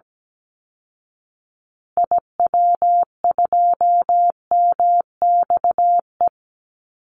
Morse code, heard as a keyed tone, about 17 wpm, 700 Hz. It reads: IW2MXE
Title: Call Signs Repeated 2x - 17wpm